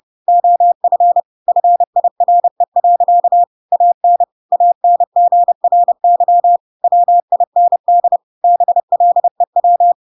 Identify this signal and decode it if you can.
Morse code, heard as a keyed tone, about 30 wpm, 700 Hz. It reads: OF FIRE. AN ANGRY WIND BLEW
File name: Pinocchio / Pinocchio ch6 30 30 ik1hge